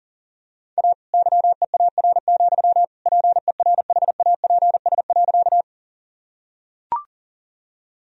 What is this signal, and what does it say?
Morse code, heard as a keyed tone, about 40 wpm, 700 Hz. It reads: A YEAR, PERHAPS.
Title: Sentences from Top 1000 Words - Review Part 1 - 40wpm